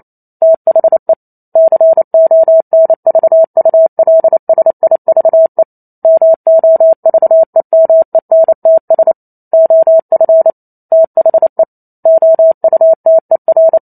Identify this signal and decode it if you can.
Morse code, heard as a keyed tone, about 28 wpm, 650 Hz. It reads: THE CONVULSIVE MOVEMENTS OF THE OUTER